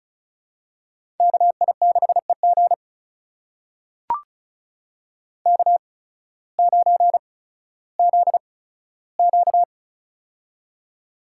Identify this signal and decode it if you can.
Morse code, heard as a keyed tone, about 36 wpm, spaced as for 16 wpm, 700 Hz. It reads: KI6EG K9ZQ
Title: USA Call Signs - 8x Character Spacing - 35wpm